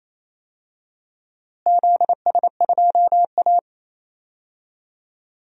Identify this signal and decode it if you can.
Morse code, heard as a keyed tone, about 28 wpm, 700 Hz. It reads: ZS2A
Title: DXCC-100 Call Signs - 4 Characters - Repeated 2x - 28wpm